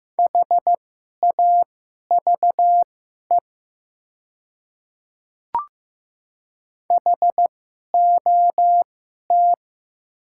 Morse code, heard as a keyed tone, about 15 wpm, spaced as for 11 wpm, 700 Hz. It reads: HAVE HOT
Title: 2x Character Spacing - Top 100 Words - 15wpm